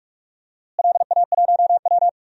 Morse code, heard as a keyed tone, about 46 wpm, 700 Hz. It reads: LA1W
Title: DXCC-100 Call Signs - 4 Characters - Repeated 3x - 45wpm